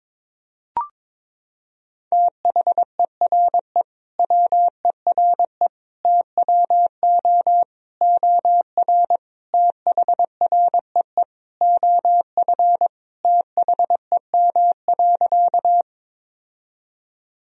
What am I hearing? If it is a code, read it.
Morse code, heard as a keyed tone, about 22 wpm, 700 Hz. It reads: THERE WERE TWO OR THREE OF THEM.
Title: Sentences from Top 1000 Words - Review Part 1 - 22wpm